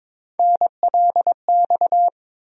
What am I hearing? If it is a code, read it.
Morse code, heard as a keyed tone, about 22 wpm, 700 Hz. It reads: NLX
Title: USA Call Sign Postfixes - 3 Letters 22wpm